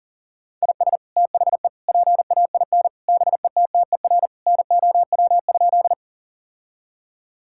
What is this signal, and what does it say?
Morse code, heard as a keyed tone, about 37 wpm, 700 Hz. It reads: IS THE PAIN BETTER NOW?
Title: Sentences from Top 1600 Words 40wpm